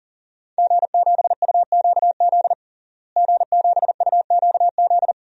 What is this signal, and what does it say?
Morse code, heard as a keyed tone, about 40 wpm, 700 Hz. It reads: G7UQZ G7UQZ
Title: DXCC-100 Call Signs - 5 Characters - Repeated 2x - 40wpm